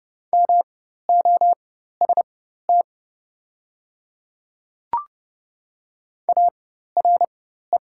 Morse code, heard as a keyed tone, about 30 wpm, spaced as for 14 wpm, 700 Hz. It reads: MOST ARE
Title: 4x Character Spacing - Top 100 Words - 30wpm